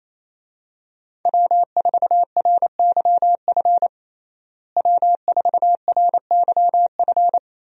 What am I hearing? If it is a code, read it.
Morse code, heard as a keyed tone, about 28 wpm, 700 Hz. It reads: W4RYF W4RYF
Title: USA Call Signs - 5 Characters - Repeated 2x - 28wpm